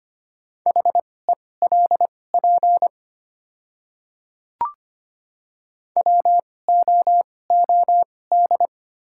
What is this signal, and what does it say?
Morse code, heard as a keyed tone, about 25 wpm, spaced as for 18 wpm, 700 Hz. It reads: HELP WOOD
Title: 2x Character Spacing - Top 500 Words - 25wpm